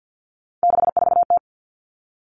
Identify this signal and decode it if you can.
Morse code, heard as a keyed tone, about 50 wpm, 700 Hz. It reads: 64T
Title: Common Cut Numbers - 3-Digit 50wpm